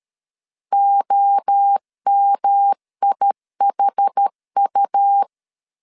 Morse code, heard as a keyed tone, about 13 wpm, 800 Hz. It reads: OMIHU
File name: CW / DET / 1 3 1500C S C 12wpm 2750